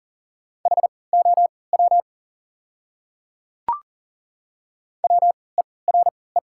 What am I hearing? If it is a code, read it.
Morse code, heard as a keyed tone, about 40 wpm, spaced as for 23 wpm, 700 Hz. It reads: HOW WERE
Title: 3x Character Spacing - Top 100 Words - 40wpm